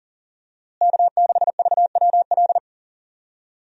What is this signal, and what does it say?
Morse code, heard as a keyed tone, about 40 wpm, 700 Hz. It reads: K6VWL
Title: USA Call Signs - 5 Characters - Repeated 2x - 40wpm